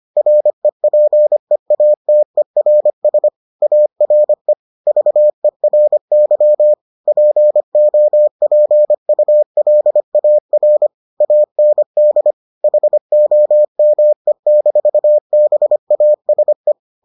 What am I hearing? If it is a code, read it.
Morse code, heard as a keyed tone, about 25 wpm, 600 Hz. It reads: REPEATERS ARE VERY POPULAR AND HOME-BASE